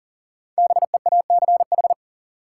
Morse code, heard as a keyed tone, about 38 wpm, 700 Hz. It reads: BEACH
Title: Five Letter Words - 40wpm